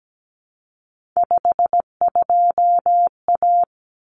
Morse code, heard as a keyed tone, about 17 wpm, 700 Hz. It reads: 52A
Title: Common Cut Numbers - 3-Digit 17wpm